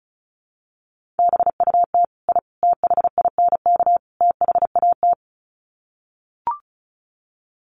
Morse code, heard as a keyed tone, about 34 wpm, 700 Hz. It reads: BUT I THINK THAT
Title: Sets of 4 Words 35wpm